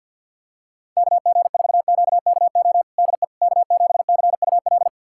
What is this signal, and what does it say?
Morse code, heard as a keyed tone, about 53 wpm, 700 Hz. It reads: KG4QKO DE K7CUL
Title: USA Callsign DE Callsign 50wpm